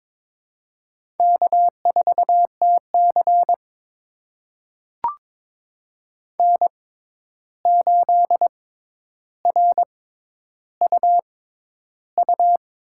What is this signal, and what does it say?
Morse code, heard as a keyed tone, about 22 wpm, spaced as for 11 wpm, 700 Hz. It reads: K4TC N8RUU
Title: USA Call Signs - 6x Character Spacing - 22wpm